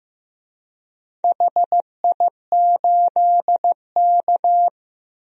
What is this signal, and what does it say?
Morse code, heard as a keyed tone, about 15 wpm, 700 Hz. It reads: HI8K
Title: DXCC-100 Call Signs - 4 Characters - 15wpm